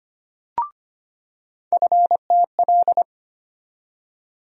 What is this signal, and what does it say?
Morse code, heard as a keyed tone, about 25 wpm, 700 Hz. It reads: FTL
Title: USA Call Sign Postfixes - 3 Letters 25wpm